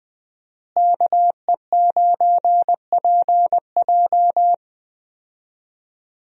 Morse code, heard as a keyed tone, about 21 wpm, 700 Hz. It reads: KE9PJ
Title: USA Call Signs - 5 Characters - Repeated 2x - 20wpm